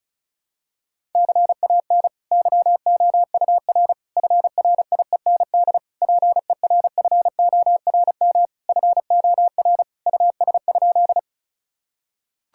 Morse code, heard as a keyed tone, about 34 wpm, 690 Hz. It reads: CAN YOUR FRIEND PERFORM FOR US?
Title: Sentences from Top 1000 Words - Review Part 5 - 35wpm